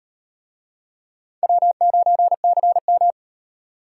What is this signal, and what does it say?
Morse code, heard as a keyed tone, about 38 wpm, 700 Hz. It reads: W9CM